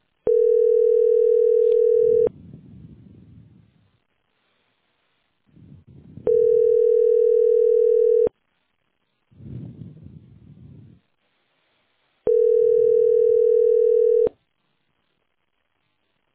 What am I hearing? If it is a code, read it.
Ringback tone (North America), 440 + 480 Hz.